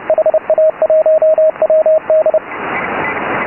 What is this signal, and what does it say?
Morse code, heard as a keyed tone, about 29 wpm, 620 Hz, with radio noise. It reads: HA1WD